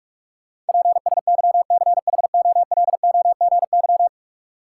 Morse code, heard as a keyed tone, about 47 wpm, 700 Hz. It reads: PSYCHOLOGY
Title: Sentences from Top 1400 Words - New Word Drill Alt - 45wpm